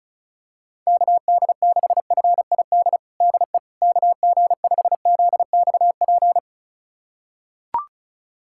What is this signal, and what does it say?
Morse code, heard as a keyed tone, about 36 wpm, 700 Hz. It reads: KD6FID DE KG5ZXP
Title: USA Callsign DE Callsign - 6 Characters - 35wpm